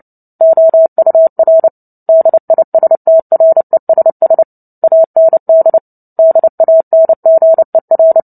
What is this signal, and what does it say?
Morse code, heard as a keyed tone, about 30 wpm, 650 Hz. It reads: OUR DISTRESS AND DANGER
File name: Book Two / 03-Chapter 3 - 29 WPM